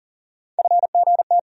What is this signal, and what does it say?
Morse code, heard as a keyed tone, about 40 wpm, 700 Hz. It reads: FGT
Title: USA Call Sign Postfixes - 3 Letters 40wpm